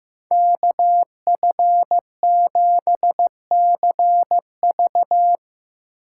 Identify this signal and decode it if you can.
Morse code, heard as a keyed tone, about 15 wpm, 700 Hz. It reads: KF7CV
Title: USA Call Signs - 4x Character Spacing - 15wpm